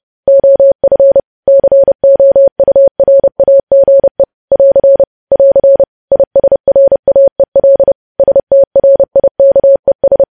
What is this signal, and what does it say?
Morse code, heard as a keyed tone, about 30 wpm, 550 Hz. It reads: OF COURAGE <AR> <AR> ISRAEL STRIKES
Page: news headlines in morse code at 30 wpm